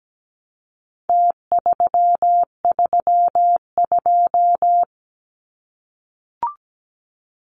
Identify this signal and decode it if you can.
Morse code, heard as a keyed tone, about 17 wpm, 700 Hz. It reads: T332
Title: Numbers - Contest Style - 4-Digit 17wpm